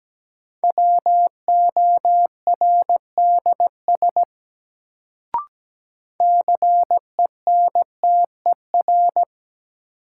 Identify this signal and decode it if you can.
Morse code, heard as a keyed tone, about 17 wpm, 700 Hz. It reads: WORDS CENTER
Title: Sets of 3 Words - New Word Drill - 17wpm